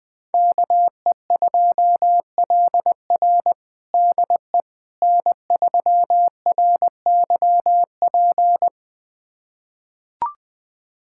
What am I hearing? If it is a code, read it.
Morse code, heard as a keyed tone, about 20 wpm, 700 Hz. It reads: KE2LR DE N3RYP
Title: USA Callsign DE Callsign - 5 Characters - 20wpm